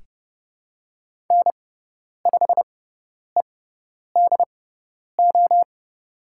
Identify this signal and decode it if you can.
Morse code, heard as a keyed tone, about 30 wpm, spaced as for 10 wpm, 700 Hz. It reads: N5EDO
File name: Lesson 17g - Letter u - Call Signs 30wpm